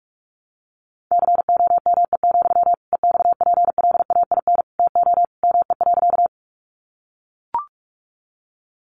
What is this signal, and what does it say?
Morse code, heard as a keyed tone, about 44 wpm, 700 Hz. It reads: COME, EXPLAIN TO ME.
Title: Sentences from Top 1100 Words 45wpm